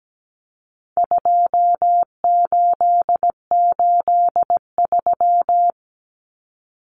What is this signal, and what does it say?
Morse code, heard as a keyed tone, about 17 wpm, 700 Hz. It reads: 2883
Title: Common Cut Numbers - 4-Digit 17wpm